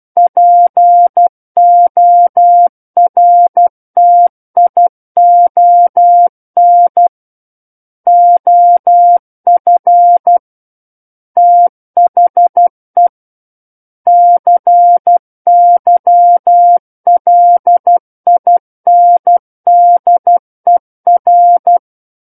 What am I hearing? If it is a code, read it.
Morse code, heard as a keyed tone, about 12 wpm, 700 Hz. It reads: PORTION OF THE CYLINDER
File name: Book One / 03-Chapter 3 - 12 WPM